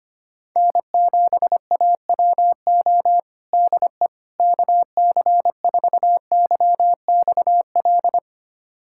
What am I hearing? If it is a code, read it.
Morse code, heard as a keyed tone, about 25 wpm, 700 Hz. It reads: N7AWO DE KC4YXL
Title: USA Callsign DE Callsign 25wpm